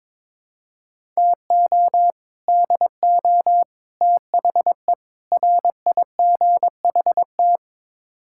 Morse code, heard as a keyed tone, about 22 wpm, 700 Hz. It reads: TO DO THE RIGHT
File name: practice / 22wpm / Build A Phrase - 5-Word Phrases - 22wpm